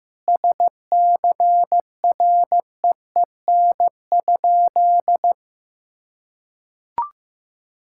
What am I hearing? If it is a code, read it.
Morse code, heard as a keyed tone, about 15 wpm, 700 Hz. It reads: SCREEN?
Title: Sentences from Top 1000 Words - Review Part 6 - 15wpm